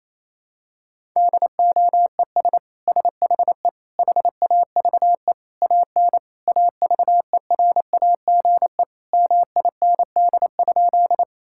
Morse code, heard as a keyed tone, about 28 wpm, 700 Hz. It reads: DOES SHE HAVE AN AVERAGE MIND?
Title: Sentences from Top 1000 Words - Review Part 6 - 28wpm